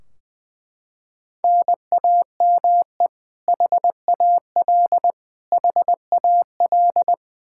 Morse code, heard as a keyed tone, about 20 wpm, 700 Hz. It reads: NAME HAL HAL
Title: Practice QSO - Exchange 1 - Beginning - 20wpm